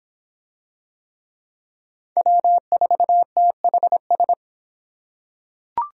Morse code, heard as a keyed tone, about 24 wpm, 700 Hz. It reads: W4THS